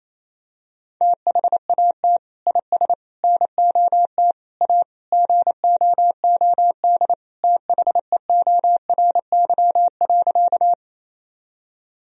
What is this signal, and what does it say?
Morse code, heard as a keyed tone, about 28 wpm, 700 Hz. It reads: THAT IS NOT A GOOD THEORY.
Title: Sentences from Top 1000 Words - Review Part 3 - 28wpm